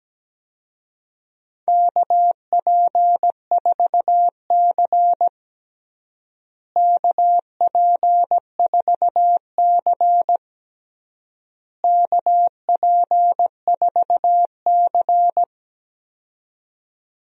Morse code, heard as a keyed tone, about 17 wpm, 700 Hz. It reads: KP4C KP4C KP4C
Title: DXCC-100 Call Signs - 4 Characters - Repeated 3x - 17wpm